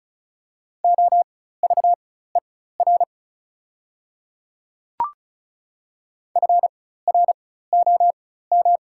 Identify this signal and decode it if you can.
Morse code, heard as a keyed tone, about 36 wpm, spaced as for 16 wpm, 700 Hz. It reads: OVER FROM
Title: 4x Character Spacing - Top 500 Words - 35wpm